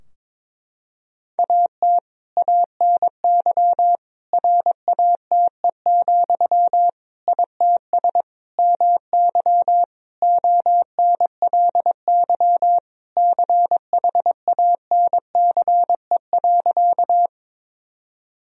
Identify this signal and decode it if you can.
Morse code, heard as a keyed tone, about 22 wpm, 700 Hz. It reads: AT ANY RATE, ITS MY ONLY CHANCE.
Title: Sentences from Top 1000 Words - Review Part 4 - 22wpm